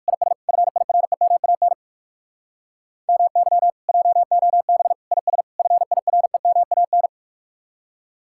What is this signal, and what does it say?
Morse code, heard as a keyed tone, about 63 wpm, spaced as for 55 wpm, 700 Hz. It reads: IS FIREMAN MY JOB IS FIREMAN
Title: Speed-Racing Common QSO Phrases - 45wpm